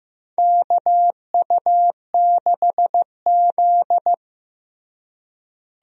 Morse code, heard as a keyed tone, about 15 wpm, 700 Hz. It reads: KU6Z
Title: USA Call Signs - 4 Characters - Repeated 3x - 15wpm